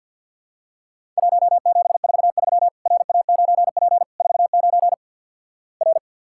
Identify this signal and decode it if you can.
Morse code, heard as a keyed tone, about 48 wpm, 700 Hz. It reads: 1743 RA9P 49 R
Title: Worked All Europe DX Contest - QTC Exchange with Repeats - 50wpm